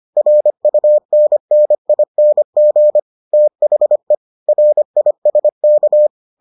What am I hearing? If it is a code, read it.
Morse code, heard as a keyed tone, about 25 wpm, 600 Hz. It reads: RUNNING THE RISK